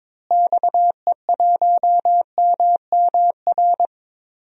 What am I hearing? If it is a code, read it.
Morse code, heard as a keyed tone, about 22 wpm, 700 Hz. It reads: XE1MMR
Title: DXCC-100 Call Signs - 6 Characters - Repeated 3x - 22wpm